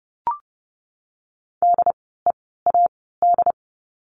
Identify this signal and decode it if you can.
Morse code, heard as a keyed tone, about 30 wpm, 700 Hz. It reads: D E A D
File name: Lesson 16f - Letter c - Words 30wpm